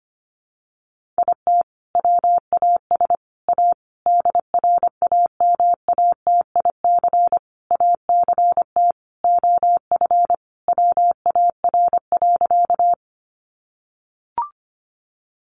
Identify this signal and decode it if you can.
Morse code, heard as a keyed tone, about 25 wpm, 700 Hz. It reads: IT WAS A DRAMATIC ACT OF WAR.
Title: Sentences from Top 1300 Words 25wpm